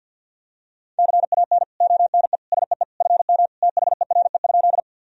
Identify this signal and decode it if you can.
Morse code, heard as a keyed tone, about 48 wpm, 700 Hz. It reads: CAN ONE SEE FM THERE?
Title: Sentences from Top 100 Words and Abbreviations 50wpm